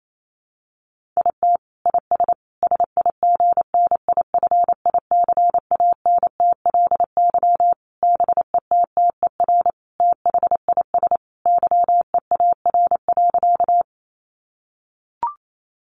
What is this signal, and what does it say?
Morse code, heard as a keyed tone, about 28 wpm, 700 Hz. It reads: IT IS SIGNIFICANTLY BETTER THIS YEAR.
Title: Sentences from Top 1500 Words 28wpm